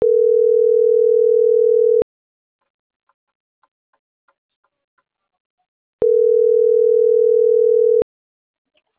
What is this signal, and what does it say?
Ringback tone (North America), 440 + 480 Hz.